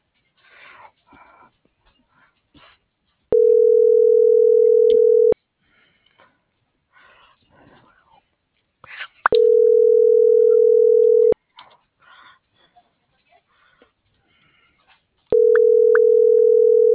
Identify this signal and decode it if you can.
Ringback tone (North America), 440 + 480 Hz.
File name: CENTRAL BLN / out ericka.canales - .37708